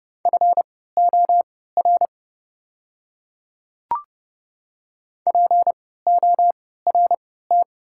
Morse code, heard as a keyed tone, about 30 wpm, spaced as for 17 wpm, 700 Hz. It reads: FOR PORT
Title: 3x Character Spacing - Top 300 Words - 30wpm